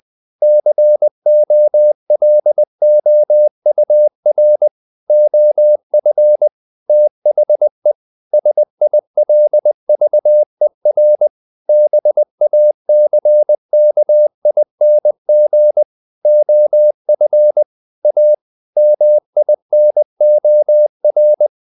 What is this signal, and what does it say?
Morse code, heard as a keyed tone, about 20 wpm, 600 Hz. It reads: COLOUR OF THE SILVER BACKING OF A MINOR
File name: the old man and the sea part 2 20wpm